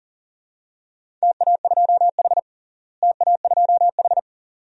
Morse code, heard as a keyed tone, about 37 wpm, 700 Hz. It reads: TA2H TA2H
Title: DXCC-100 Call Signs - 4 Characters - Repeated 2x - 40wpm